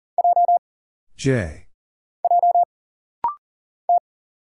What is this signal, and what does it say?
Morse code, heard as a keyed tone, about 39 wpm, 700 Hz. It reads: J J T
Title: Single Letter-Number - Rapid-Fire 40wpm